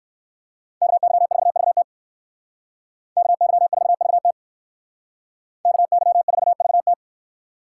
Morse code, heard as a keyed tone, about 58 wpm, 700 Hz. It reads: KY4VT KY4VT KY4VT